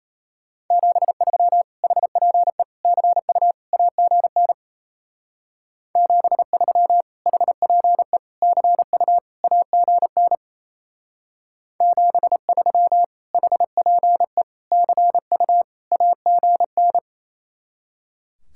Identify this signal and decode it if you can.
Morse code, heard as a keyed tone, about 33 wpm, 700 Hz. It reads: 73 HPE CU AGN 73 HPE CU AGN 73 HPE CU AGN
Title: Speed-Racing Common QSO Phrases - 28wpm